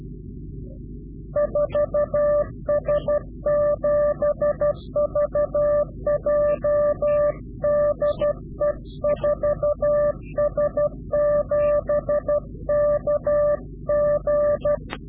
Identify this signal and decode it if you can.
Morse code, heard as a keyed tone, about 12 wpm, 590 Hz. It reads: 4S7VJ DE 4S7KG
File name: CW-QSO / Directional Call / Directional QSO